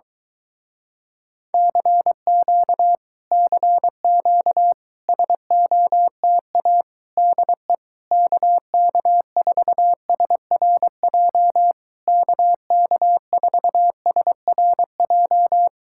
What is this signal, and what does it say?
Morse code, heard as a keyed tone, about 22 wpm, 700 Hz. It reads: CQ CQ SOTA DE KK4SRJ KK4SRJ